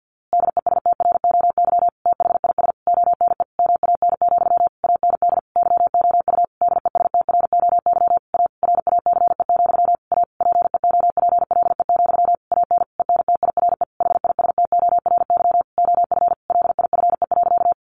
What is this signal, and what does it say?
Morse code, heard as a keyed tone, about 48 wpm, 700 Hz. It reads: DESTROY THIS ONE MAN, AND YOU DESTROY A RACE, A PEOPLE, AN ENTIRE HISTORY OF LIFE.
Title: A Sound Of Thunder 50wpm